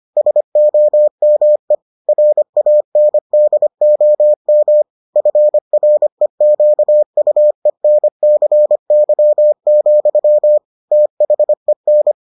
Morse code, heard as a keyed tone, about 25 wpm, 600 Hz. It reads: SOME RANDOM FREQUENCY, THEN